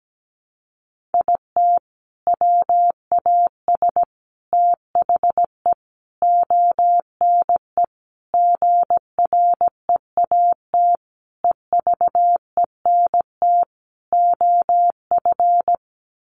Morse code, heard as a keyed tone, about 17 wpm, 700 Hz. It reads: IT WAS THE ONE GREAT EVENT OF
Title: Sentences from Top 1400 Words 17wpm